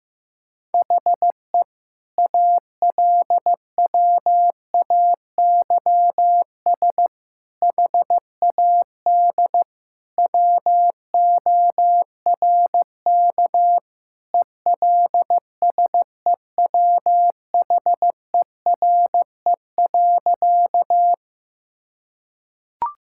Morse code, heard as a keyed tone, about 14 wpm, 700 Hz. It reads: HE ALWAYS HAD WORK ELSEWHERE.
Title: Sentences from Top 1600 Words 15wpm